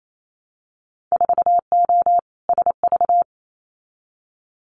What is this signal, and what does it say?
Morse code, heard as a keyed tone, about 28 wpm, 700 Hz. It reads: 4O SV
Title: Winter Field Day Exchanges 28wpm